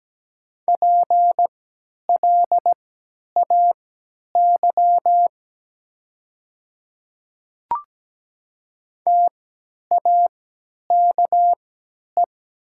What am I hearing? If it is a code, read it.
Morse code, heard as a keyed tone, about 17 wpm, spaced as for 10 wpm, 700 Hz. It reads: PLAY TAKE
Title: 3x Character Spacing - Top 300 Words - 17wpm